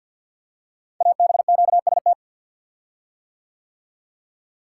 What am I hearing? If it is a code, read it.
Morse code, heard as a keyed tone, about 50 wpm, 700 Hz. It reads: ABQST
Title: Code Groups - 5 Letters - 50wpm